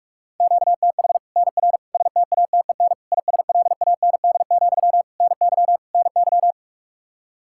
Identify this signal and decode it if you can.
Morse code, heard as a keyed tone, about 44 wpm, 700 Hz. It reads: QTH NR STATEN ISLAND, NY NY
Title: Practice QSO - Exchange 1 - Beginning - 45wpm